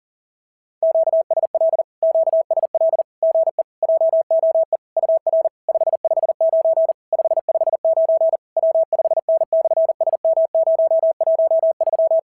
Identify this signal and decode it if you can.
Morse code, heard as a keyed tone, about 37 wpm, 660 Hz. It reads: QSL QSL GE JOE UR 559 559 W5N/SM013
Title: SOTA QSO Practice 40wpm